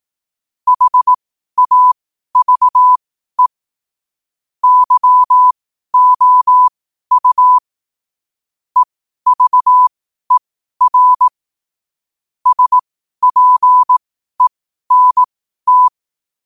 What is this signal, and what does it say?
Morse code, heard as a keyed tone, about 18 wpm, spaced as for 12 wpm, 1000 Hz. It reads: HAVE YOU EVER SPENT
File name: qotdfiles / 12wpm